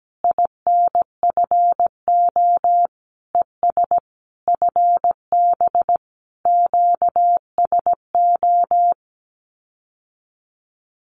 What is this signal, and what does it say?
Morse code, heard as a keyed tone, about 17 wpm, 700 Hz. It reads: INFO ES FB QSO
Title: Common QSO Phrases Encore 17wpm